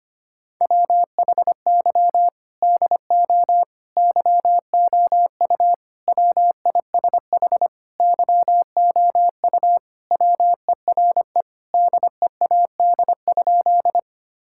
Morse code, heard as a keyed tone, about 25 wpm, 700 Hz. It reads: WHY DO YOU WISH YOU WERE DEAD?
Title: Sentences from Top 1600 Words 25wpm